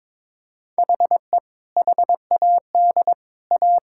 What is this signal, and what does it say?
Morse code, heard as a keyed tone, about 21 wpm, 700 Hz. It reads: HE HAD A
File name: practice / 22wpm / Sentences with 3-Word Phrases - 22wpm